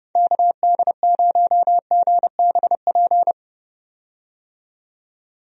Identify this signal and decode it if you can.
Morse code, heard as a keyed tone, about 32 wpm, 700 Hz. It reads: KD0GBP
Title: USA Call Signs - 6 Characters - 30wpm